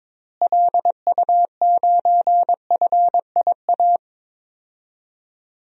Morse code, heard as a keyed tone, about 22 wpm, 700 Hz. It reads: LU9FIA
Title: DXCC-100 Call Signs - 6 Characters - Repeated 3x - 22wpm